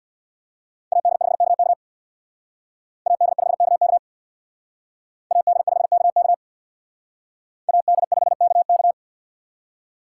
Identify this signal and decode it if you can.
Morse code, heard as a keyed tone, about 67 wpm, 700 Hz. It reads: AD5KK AD5KK AD5KK AD5KK